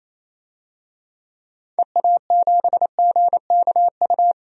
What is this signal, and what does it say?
Morse code, heard as a keyed tone, about 28 wpm, 700 Hz. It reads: EA7GKU
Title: DXCC-100 Call Signs - 6 Characters - 28wpm